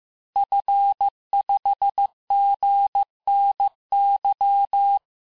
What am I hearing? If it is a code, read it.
Morse code, heard as a keyed tone, about 15 wpm, 800 Hz. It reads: F5GNY
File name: cours-du-lundi files / cours-1585-texte-FIN-de-QSO-vitesse-14mpm